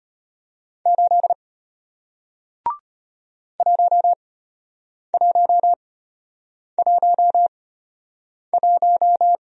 Morse code, heard as a keyed tone, about 34 wpm, spaced as for 8 wpm, 700 Hz. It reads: 8 1111